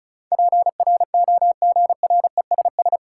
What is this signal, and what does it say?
Morse code, heard as a keyed tone, about 36 wpm, 700 Hz. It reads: PROGRESS
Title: Sentences from Top 1400 Words - New Word Drill Alt - 35wpm